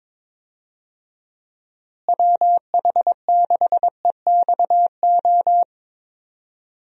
Morse code, heard as a keyed tone, about 22 wpm, 700 Hz. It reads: WH6EXO
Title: USA Call Signs - 6 Characters - Repeated 3x - 22wpm